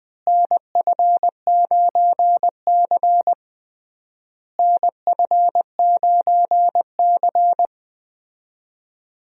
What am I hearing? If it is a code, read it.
Morse code, heard as a keyed tone, about 20 wpm, 700 Hz. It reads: NF9C NF9C
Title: USA Call Signs - 4 Characters - Repeated 3x - 20wpm